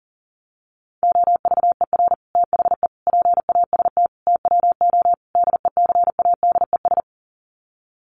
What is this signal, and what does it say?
Morse code, heard as a keyed tone, about 37 wpm, 700 Hz. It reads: OVER THE PAST TWO DECADES
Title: Sets of 5 Words Encore2 40wpm